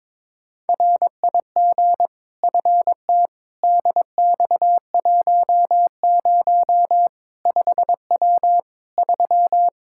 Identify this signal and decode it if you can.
Morse code, heard as a keyed tone, about 22 wpm, 700 Hz. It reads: RIG FT DX10 5W 3
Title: Practice QSO - Exchange 2 - Info - 22wpm